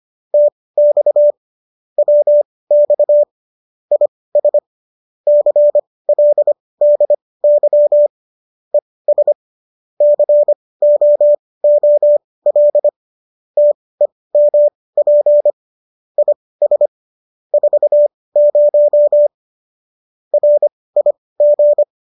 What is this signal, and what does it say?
Morse code, heard as a keyed tone, about 25 wpm, spaced as for 18 wpm, 590 Hz. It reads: TX WX IS CLDY ES COOL TEMP IS 40 RIG